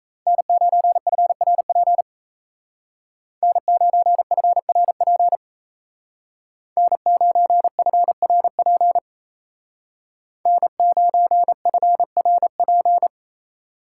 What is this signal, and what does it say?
Morse code, heard as a keyed tone, about 35 wpm, 700 Hz. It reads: N9FRP N9FRP N9FRP N9FRP
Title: Speed Racing - USA 1x3 Call Signs 28wpm